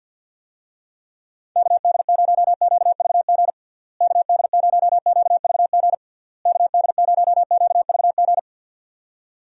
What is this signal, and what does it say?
Morse code, heard as a keyed tone, about 50 wpm, 700 Hz. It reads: KD0QVG KD0QVG KD0QVG
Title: USA Call Signs - 6 Characters - Repeated 3x - 50wpm